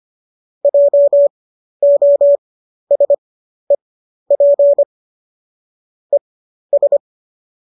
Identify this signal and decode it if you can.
Morse code, heard as a keyed tone, about 25 wpm, spaced as for 12 wpm, 570 Hz. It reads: JOSEP ES
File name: qso009 12